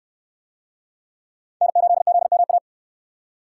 Morse code, heard as a keyed tone, about 61 wpm, 700 Hz. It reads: N8ZMA